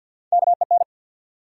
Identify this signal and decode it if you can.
Morse code, heard as a keyed tone, about 52 wpm, 700 Hz. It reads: KEN
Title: USA Common Names - Max 4 Letters - 50wpm